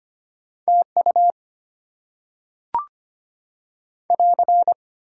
Morse code, heard as a keyed tone, about 25 wpm, 700 Hz. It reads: TU <AR>
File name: Top 90 QSO Elements - 8x Character Spacing - 25wpm